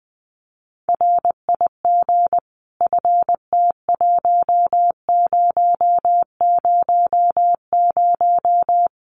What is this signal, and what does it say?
Morse code, heard as a keyed tone, about 20 wpm, 700 Hz. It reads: RIG FT1000
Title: Common QSO Phrases 20wpm